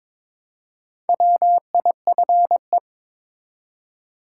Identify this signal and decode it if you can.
Morse code, heard as a keyed tone, about 21 wpm, 700 Hz. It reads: WIFE
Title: Four Letter Words - 22wpm